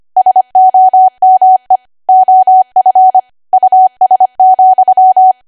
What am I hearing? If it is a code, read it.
Morse code, heard as a keyed tone, about 25 wpm, 750 Hz. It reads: SOME OF US,
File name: Archive / 25 WPM / 25WPM